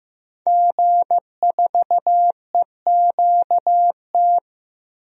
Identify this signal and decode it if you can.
Morse code, heard as a keyed tone, about 15 wpm, 700 Hz. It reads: G4EQT
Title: DXCC-100 Call Signs - 5 Characters - Repeated 2x - 15wpm